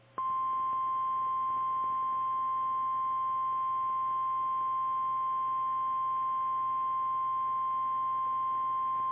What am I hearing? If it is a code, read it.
Emergency Alert System (SAME) tones (the 1050 Hz attention tone). It is heard from 0.9 to 8.6 s.
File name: eas / KKPRKXPN / KKPRKXPN